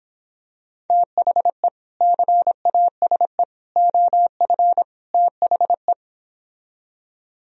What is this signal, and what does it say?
Morse code, heard as a keyed tone, about 26 wpm, 700 Hz. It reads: THE CASE OF THE